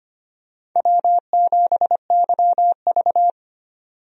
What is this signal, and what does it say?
Morse code, heard as a keyed tone, about 25 wpm, 700 Hz. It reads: W7YV
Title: USA Call Signs - 4 Characters - 25wpm